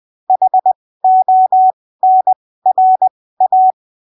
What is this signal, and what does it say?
Morse code, heard as a keyed tone, about 20 wpm, spaced as for 15 wpm, 750 Hz. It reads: HONRA
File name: arquivos / texto / texto 28 20ppm